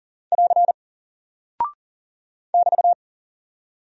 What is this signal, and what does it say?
Morse code, heard as a keyed tone, about 38 wpm, 700 Hz. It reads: <AR> <BT>
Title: Single Character (Letter:Number:Punctuation:Prosign) Alt 40wpm